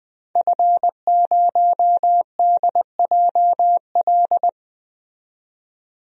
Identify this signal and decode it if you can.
Morse code, heard as a keyed tone, about 21 wpm, 700 Hz. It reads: F0DJL
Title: DXCC-100 Call Signs - 5 Characters - 20wpm